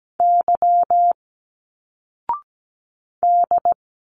Morse code, heard as a keyed tone, about 17 wpm, 700 Hz. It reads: Y D
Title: Single Letters 17wpm